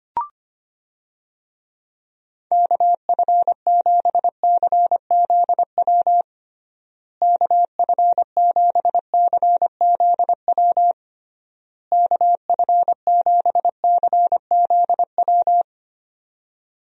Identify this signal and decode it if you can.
Morse code, heard as a keyed tone, about 25 wpm, 700 Hz. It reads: KF7CZW KF7CZW KF7CZW
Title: USA Call Signs - 6 Characters - Repeated 3x - 25wpm